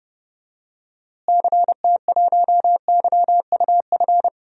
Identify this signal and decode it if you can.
Morse code, heard as a keyed tone, about 30 wpm, 700 Hz. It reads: CT1YUF
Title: DXCC-100 Call Signs - 6 Characters - Repeated 3x - 30wpm